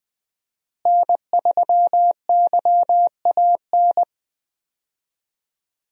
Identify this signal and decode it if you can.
Morse code, heard as a keyed tone, about 20 wpm, 700 Hz. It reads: N3YAN
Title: USA Call Signs - 5 Characters - Repeated 3x - 20wpm